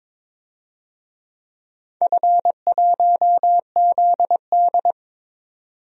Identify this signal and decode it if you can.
Morse code, heard as a keyed tone, about 22 wpm, 700 Hz. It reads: F1ZD
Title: DXCC-100 Call Signs - 4 Characters - 22wpm